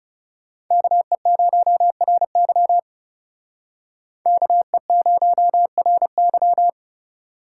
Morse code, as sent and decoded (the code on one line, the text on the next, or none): -.- . ----- .-. -.-- / -.- . ----- .-. -.--
KE0RY KE0RY